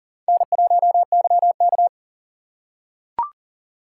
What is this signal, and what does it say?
Morse code, heard as a keyed tone, about 40 wpm, 700 Hz. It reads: N1YK
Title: USA Call Signs - 4 Characters - 6x Character Spacing - 40wpm